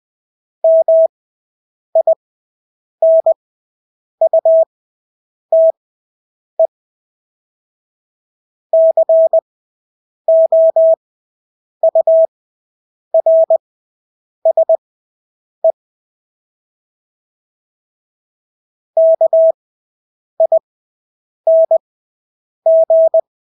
Morse code, heard as a keyed tone, about 20 wpm, spaced as for 8 wpm, 650 Hz. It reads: MINUTE COURSE KING
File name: Engelske ord 20/8 wpm